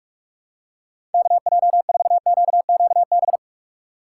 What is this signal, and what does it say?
Morse code, heard as a keyed tone, about 46 wpm, 700 Hz. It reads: KJ4QQB
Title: USA Call Signs - 4x Character Spacing - 45wpm